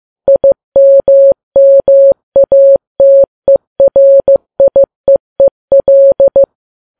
Morse code, heard as a keyed tone, about 15 wpm, 550 Hz. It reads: IMMATERIEEL
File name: Files Web tls / Morse on9fv / morse15kps